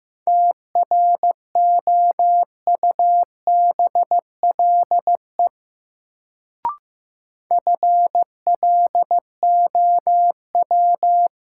Morse code, heard as a keyed tone, about 15 wpm, 700 Hz. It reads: TROUBLE FLOW
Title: Sentences from Top 1200 Words - New Word Drill - 15wpm